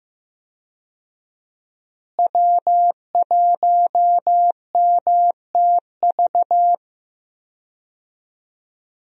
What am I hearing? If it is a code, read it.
Morse code, heard as a keyed tone, about 16 wpm, 700 Hz. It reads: W1MTV
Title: USA Call Signs - 5 Characters - Repeated 3x - 15wpm